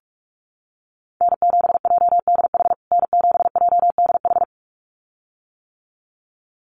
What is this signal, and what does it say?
Morse code, heard as a keyed tone, about 46 wpm, 700 Hz. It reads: N7JDH N7JDH
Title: USA Call Signs Repeated 2x 45wpm